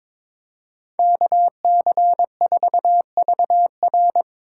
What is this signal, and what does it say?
Morse code, heard as a keyed tone, about 22 wpm, 700 Hz. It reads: KC4VR
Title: USA Call Signs - 5 Characters - 4x Character Spacing - 22wpm